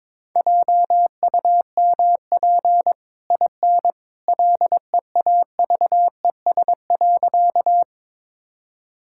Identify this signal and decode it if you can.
Morse code, heard as a keyed tone, about 22 wpm, 690 Hz. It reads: JUMP IN LEAVES.
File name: practice / 22wpm / Easy Sentences with Hints - 22wpm